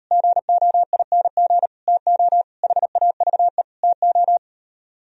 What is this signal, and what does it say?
Morse code, heard as a keyed tone, about 38 wpm, 700 Hz. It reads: GOING TO HAVE TO